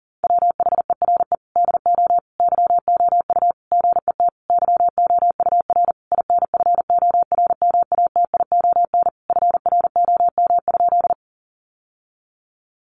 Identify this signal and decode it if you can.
Morse code, heard as a keyed tone, about 40 wpm, 700 Hz. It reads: WHERE DO YOU GET YOUR INFORMATION FROM?
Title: Sentences from Top 600 Words 40wpm